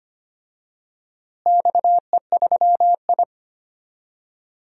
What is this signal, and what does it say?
Morse code, heard as a keyed tone, about 25 wpm, 700 Hz. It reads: XE3I
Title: DXCC-100 Call Signs - 4 Characters - Repeated 2x - 25wpm